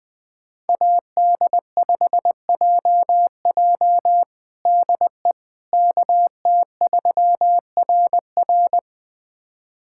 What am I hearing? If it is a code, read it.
Morse code, heard as a keyed tone, about 20 wpm, 700 Hz. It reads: AD5JJ DE KT3RR
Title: USA Callsign DE Callsign - 5 Characters - 20wpm